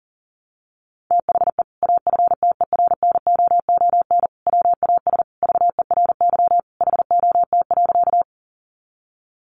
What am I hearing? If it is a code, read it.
Morse code, heard as a keyed tone, about 40 wpm, 700 Hz. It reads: THE AFTERNOON WAS VERY HOT.
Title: Sentences from Top 1500 Words 40wpm